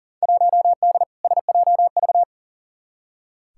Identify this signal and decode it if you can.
Morse code, heard as a keyed tone, about 40 wpm, 700 Hz. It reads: 1D SJV
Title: Field Day Exchanges Encore - 40wpm